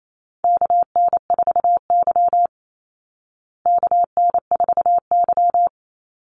Morse code, heard as a keyed tone, about 28 wpm, 700 Hz. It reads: KN4Y KN4Y
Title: Call Signs Repeated 2x - 28wpm